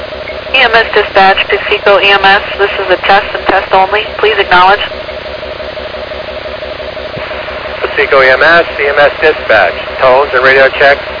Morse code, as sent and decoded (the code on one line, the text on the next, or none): none